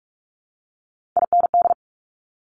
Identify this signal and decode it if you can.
Morse code, heard as a keyed tone, about 46 wpm, 700 Hz. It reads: IND
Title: Top Three Letter Combinations in Words 45wpm